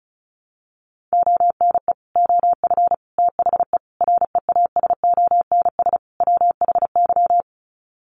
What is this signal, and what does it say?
Morse code, heard as a keyed tone, about 34 wpm, 700 Hz. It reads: ONE OF THE REASONS WHY
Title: Sets of 5 Words 35wpm